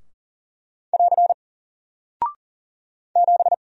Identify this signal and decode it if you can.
Morse code, heard as a keyed tone, about 40 wpm, 700 Hz. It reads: <AR> 7
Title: Single Character (Letter:Number:Punctuation:Prosign) Alt 40wpm